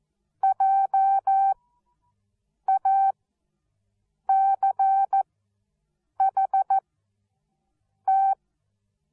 Morse code, heard as a keyed tone, about 14 wpm, spaced as for 6 wpm, 780 Hz. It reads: JACHT
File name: Onlinelehrgang / morsen / DJ4UF Morsen13